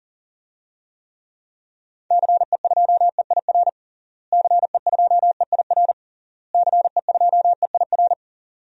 Morse code, heard as a keyed tone, about 40 wpm, 700 Hz. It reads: CE2EIR CE2EIR CE2EIR
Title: DXCC-100 Call Signs - 6 Characters - Repeated 3x - 40wpm